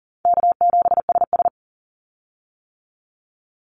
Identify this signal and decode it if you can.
Morse code, heard as a keyed tone, about 40 wpm, 700 Hz. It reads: K7SS
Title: Call Signs - 40wpm